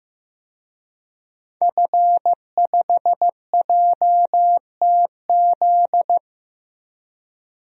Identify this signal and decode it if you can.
Morse code, heard as a keyed tone, about 15 wpm, 700 Hz. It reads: F5JTZ
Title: DXCC-100 Call Signs - 5 Characters - Repeated 2x - 15wpm